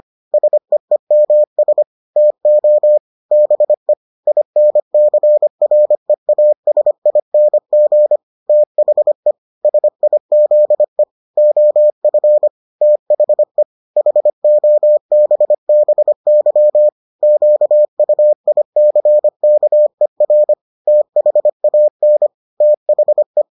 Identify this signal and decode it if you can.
Morse code, heard as a keyed tone, about 25 wpm, 600 Hz. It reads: SEEMS TO BE INCREASING THE SIZE OF THE HOBBY QUICKER THAN THE